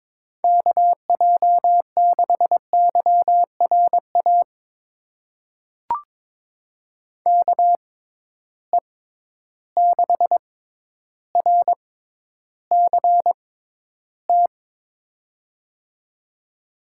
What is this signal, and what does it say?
Morse code, heard as a keyed tone, about 22 wpm, spaced as for 11 wpm, 700 Hz. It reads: KJ6YRA KE6RCT
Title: USA Call Signs - 6 Characters - 6x Character Spacing - 22wpm